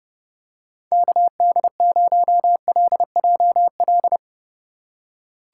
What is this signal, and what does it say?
Morse code, heard as a keyed tone, about 30 wpm, 700 Hz. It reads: KD0LJL
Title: USA Call Signs - 4x Character Spacing - 30wpm